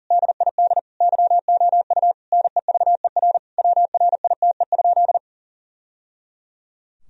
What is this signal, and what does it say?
Morse code, heard as a keyed tone, about 40 wpm, 700 Hz. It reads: DID YOU NEVER WRITE?
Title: Sentences from Top 1000 Words - Review Part 1 - 40wpm